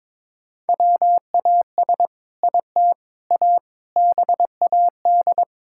Morse code, heard as a keyed tone, about 22 wpm, 700 Hz. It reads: WAS IT A BAD
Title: Sentences from Top 1000 Words - Review Part 5 - 22wpm